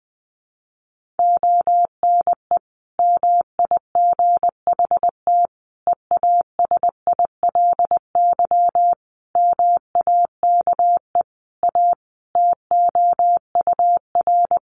Morse code, heard as a keyed tone, about 20 wpm, 700 Hz. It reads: ONE MIGHT EASILY MAKE A TOUR
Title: Sentences from Top 1400 Words 20wpm